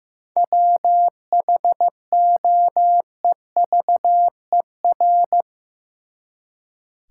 Morse code, heard as a keyed tone, about 15 wpm, 700 Hz. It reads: WHOEVER
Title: Seven Letter Words - 15wpm